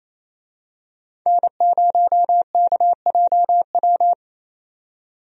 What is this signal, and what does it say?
Morse code, heard as a keyed tone, about 27 wpm, 700 Hz. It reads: N0KJW
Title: USA Call Signs - 5 Characters - 28wpm